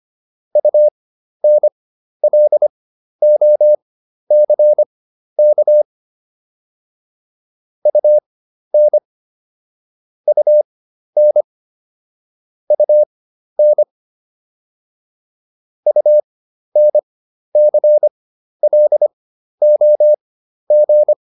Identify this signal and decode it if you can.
Morse code, heard as a keyed tone, about 25 wpm, spaced as for 12 wpm, 600 Hz. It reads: UNLOCK UN UN UN UNCLOG
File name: UN6 12